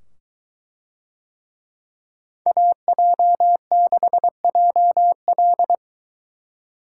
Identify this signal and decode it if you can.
Morse code, heard as a keyed tone, about 21 wpm, 700 Hz. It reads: AJ6JL